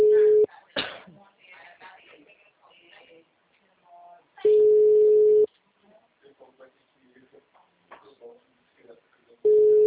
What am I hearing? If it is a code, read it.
Ringback tone (Europe, 425 Hz), with speech.